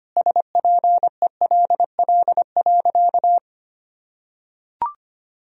Morse code, heard as a keyed tone, about 25 wpm, 700 Hz. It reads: SPELL.
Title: Sentences from Top 1000 Words - Review Part 1 - 25wpm